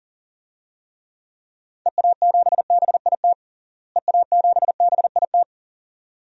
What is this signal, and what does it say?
Morse code, heard as a keyed tone, about 37 wpm, 700 Hz. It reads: EA7BIT EA7BIT
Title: DXCC-100 Call Signs - 6 Characters - Repeated 2x - 40wpm